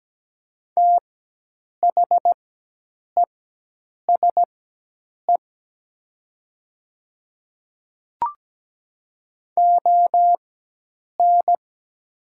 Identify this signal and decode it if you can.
Morse code, heard as a keyed tone, about 17 wpm, spaced as for 8 wpm, 700 Hz. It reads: THESE ON
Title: 4x Character Spacing - Top 100 Words - 17wpm